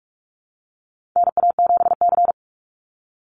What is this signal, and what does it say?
Morse code, heard as a keyed tone, about 44 wpm, 700 Hz. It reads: NA7C
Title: Call Signs Repeated 2x - 45wpm